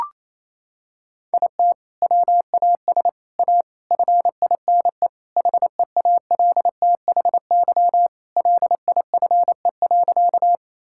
Morse code, heard as a keyed tone, about 28 wpm, 700 Hz. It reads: IT WAS A FINE HEALTHY LIFE.
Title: Sentences from Top 1000 Words - Review Part 5 - 28wpm